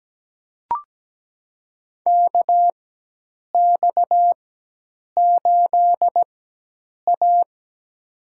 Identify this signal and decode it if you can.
Morse code, heard as a keyed tone, about 17 wpm, spaced as for 8 wpm, 700 Hz. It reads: KX8A
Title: USA Call Signs - 4 Characters - 4x Character Spacing - 17wpm